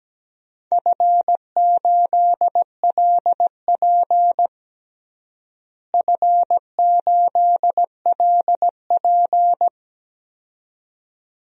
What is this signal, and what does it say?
Morse code, heard as a keyed tone, about 17 wpm, 700 Hz. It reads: F8LP F8LP
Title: DXCC-100 Call Signs - 4 Characters - Repeated 2x - 17wpm